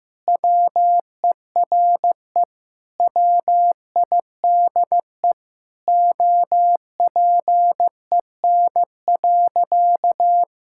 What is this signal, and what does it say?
Morse code, heard as a keyed tone, about 14 wpm, 700 Hz. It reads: WERE WIDE OPEN.
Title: Sentences from Top 1000 Words - Review Part 6 - 15wpm